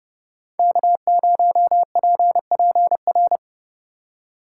Morse code, heard as a keyed tone, about 30 wpm, 700 Hz. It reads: K0PPR